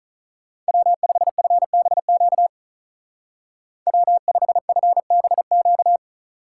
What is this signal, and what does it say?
Morse code, heard as a keyed tone, about 44 wpm, 700 Hz. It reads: W5FBQ W5FBQ
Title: Speed Racing - USA 1x3 Call Signs 35wpm